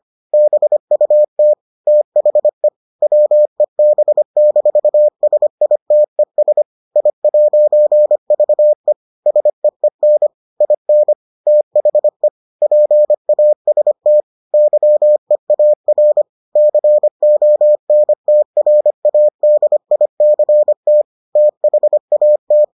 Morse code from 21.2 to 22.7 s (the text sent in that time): THAT